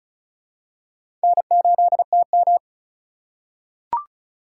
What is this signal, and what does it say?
Morse code, heard as a keyed tone, about 36 wpm, 700 Hz. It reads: N8TM
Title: USA Call Signs - 4 Characters - 2x Character Spacing - 35wpm